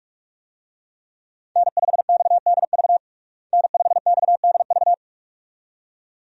Morse code, heard as a keyed tone, about 46 wpm, 700 Hz. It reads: N5XDV N5XDV
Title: USA Call Signs - 5 Characters - Repeated 2x - 45wpm